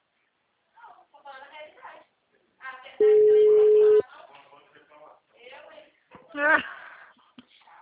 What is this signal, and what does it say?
Ringback tone (Europe, 425 Hz), with speech.
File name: out- -2034- - - .8188